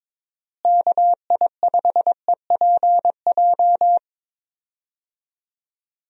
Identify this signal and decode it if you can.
Morse code, heard as a keyed tone, about 22 wpm, 700 Hz. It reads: KI5EPJ
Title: USA Call Signs - 6 Characters - Repeated 2x - 22wpm